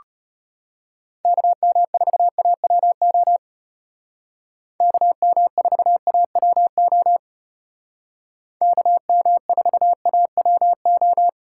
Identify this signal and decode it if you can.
Morse code, heard as a keyed tone, about 32 wpm, 700 Hz. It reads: KM4AWO KM4AWO KM4AWO